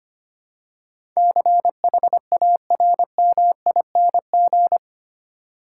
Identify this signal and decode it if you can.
Morse code, heard as a keyed tone, about 25 wpm, 700 Hz. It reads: CHARMING
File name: Eight Letter Words - 25wpm